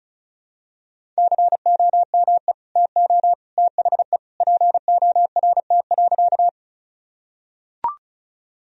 Morse code, heard as a keyed tone, about 36 wpm, 700 Hz. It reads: COME TO THE PORT.
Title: Sentences from Top 1000 Words - Review Part 1 - 35wpm